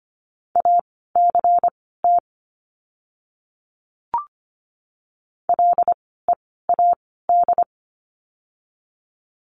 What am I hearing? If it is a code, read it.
Morse code, heard as a keyed tone, about 25 wpm, spaced as for 16 wpm, 700 Hz. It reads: ACT LEAD
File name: course / 25:10wpm / 094 Lesson 16f - Letter c - Words 25wpm